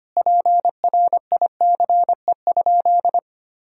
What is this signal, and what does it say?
Morse code, heard as a keyed tone, about 25 wpm, 700 Hz. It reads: PRICE?
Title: Sentences from Top 1000 Words - Review Part 4 - 25wpm